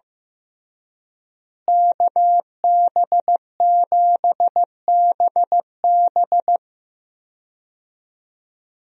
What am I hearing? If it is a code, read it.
Morse code, heard as a keyed tone, about 15 wpm, 700 Hz. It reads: KB7BB
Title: USA Call Signs - 5 Characters - Repeated 2x - 15wpm